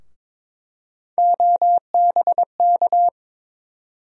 Morse code, heard as a keyed tone, about 22 wpm, 700 Hz. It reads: OBK
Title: USA Call Sign Postfixes - 3 Letters 22wpm